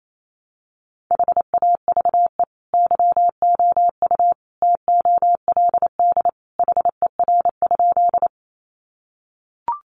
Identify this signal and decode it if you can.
Morse code, heard as a keyed tone, about 28 wpm, 700 Hz. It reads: HAVE YOU TOLD HER?
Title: Sentences from Top 400 Words 28wpm